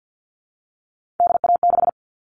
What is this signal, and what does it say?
Morse code, heard as a keyed tone, about 50 wpm, 700 Hz. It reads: DA6
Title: All Cut Numbers - 3-Digit 50wpm